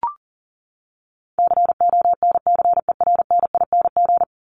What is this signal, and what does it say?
Morse code, heard as a keyed tone, about 40 wpm, 700 Hz. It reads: CONCERNING
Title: Top 1751-2000 Words 40wpm